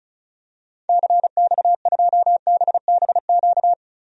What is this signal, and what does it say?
Morse code, heard as a keyed tone, about 36 wpm, 700 Hz. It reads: CX2BBQ
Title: DXCC-100 Call Signs - 6 Characters - 35wpm